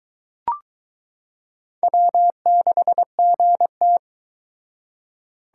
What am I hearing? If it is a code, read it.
Morse code, heard as a keyed tone, about 23 wpm, 700 Hz. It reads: W6GT